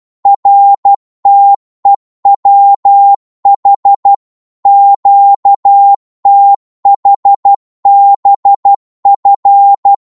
Morse code, heard as a keyed tone, about 12 wpm, 800 Hz. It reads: RTEWH QTHBF
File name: Morse / ltr-12wpm